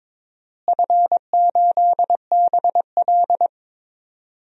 Morse code, heard as a keyed tone, about 22 wpm, 700 Hz. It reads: F8BL
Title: DXCC-100 Call Signs - 4 Characters - Repeated 2x - 22wpm